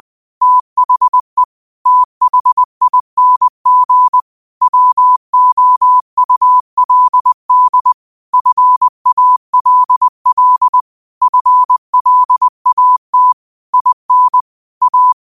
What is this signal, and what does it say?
Morse code, heard as a keyed tone, about 20 wpm, 1000 Hz. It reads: THE THING WOULD FALL FLAT IN A